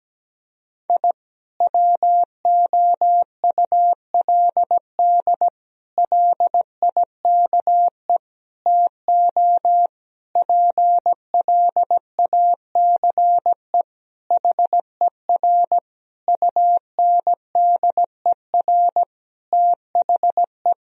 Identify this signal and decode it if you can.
Morse code, heard as a keyed tone, about 17 wpm, 700 Hz. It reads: I WOULD LIKE TO PLACE HER UNDER THE
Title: Sentences from Top 1000 Words - Review Part 1 - 17wpm